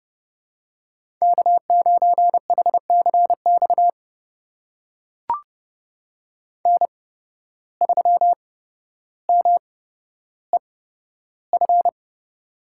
Morse code, heard as a keyed tone, about 30 wpm, spaced as for 13 wpm, 700 Hz. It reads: K9HCX N3MEF
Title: USA Call Signs - 5 Characters - 8x Character Spacing - 30wpm